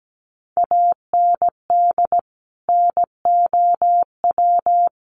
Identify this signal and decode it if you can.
Morse code, heard as a keyed tone, about 17 wpm, 700 Hz. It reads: AND NOW
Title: Sentences from Top 1400 Words 17wpm